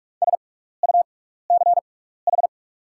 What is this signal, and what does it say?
Morse code, heard as a keyed tone, about 44 wpm, spaced as for 15 wpm, 700 Hz. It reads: SUCH
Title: 6x Character Spacing - Top 500 Words - 45wpm